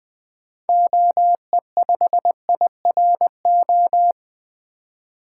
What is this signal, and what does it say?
Morse code, heard as a keyed tone, about 20 wpm, 700 Hz. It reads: OE5IRO
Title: 2023 SOTA Activator Call Signs - 20wpm